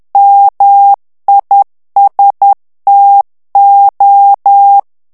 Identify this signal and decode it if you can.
Morse code, heard as a keyed tone, about 10 wpm, 800 Hz. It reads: MISTO
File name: ITALIEN texte clair 2 vitesse 10 mots minute